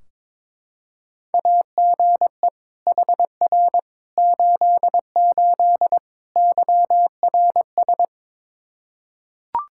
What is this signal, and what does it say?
Morse code, heard as a keyed tone, about 22 wpm, 700 Hz. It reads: AGE HR 88 YRS
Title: Practice QSO - Exchange 3 - More Info - 22wpm